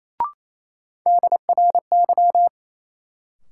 Morse code, heard as a keyed tone, about 28 wpm, 700 Hz. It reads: DRY
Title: Sentences from Top 500 Words - New Word Drill - 28wpm